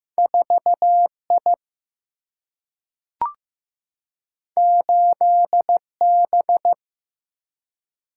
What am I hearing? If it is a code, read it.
Morse code, heard as a keyed tone, about 15 wpm, 700 Hz. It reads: 4 I 8 B
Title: Single-Letter Single-Number 15wpm